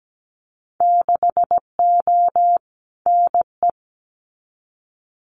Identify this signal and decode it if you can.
Morse code, heard as a keyed tone, about 17 wpm, 700 Hz. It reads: 6O NE
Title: Winter Field Day Exchanges Encore 17wpm